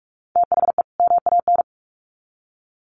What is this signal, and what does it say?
Morse code, heard as a keyed tone, about 44 wpm, 700 Hz. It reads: THE MAN
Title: Sets of 2 Words 45wpm